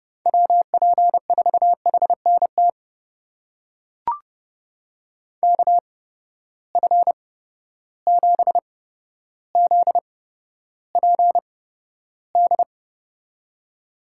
Morse code, heard as a keyed tone, about 30 wpm, spaced as for 13 wpm, 700 Hz. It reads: WP4HNT KF7ZPD
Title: USA Call Signs - 6 Characters - 8x Character Spacing - 30wpm